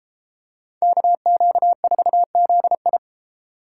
Morse code, heard as a keyed tone, about 30 wpm, 700 Hz. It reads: KQ4ZI